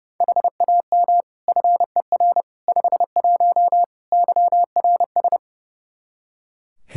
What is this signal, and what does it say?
Morse code, heard as a keyed tone, about 32 wpm, 700 Hz. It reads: HAM FER 51 YRS
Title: Practice QSO - Exchange 3 - More Info - 30wpm